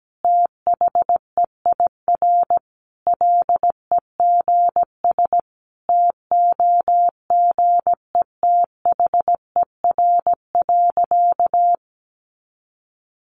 Morse code, heard as a keyed tone, about 17 wpm, 700 Hz. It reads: THEIR LEGS TOGETHER.
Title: Easy Phrases 17wpm